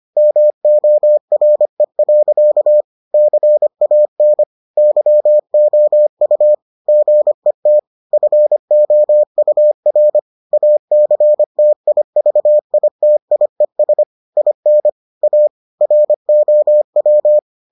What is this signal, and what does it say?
Morse code, heard as a keyed tone, about 25 wpm, 600 Hz. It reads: MORE. CAN YOU GET FOUR ACTIVITIES IN A ROW